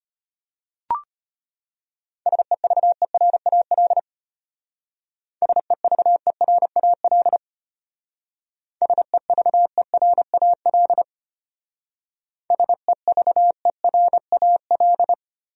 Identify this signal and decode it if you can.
Morse code, heard as a keyed tone, about 33 wpm, 700 Hz. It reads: SEVERAL SEVERAL SEVERAL SEVERAL